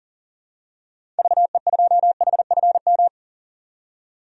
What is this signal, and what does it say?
Morse code, heard as a keyed tone, about 40 wpm, 700 Hz. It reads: VE2HFM
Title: DXCC-100 Call Signs - 6 Characters - 40wpm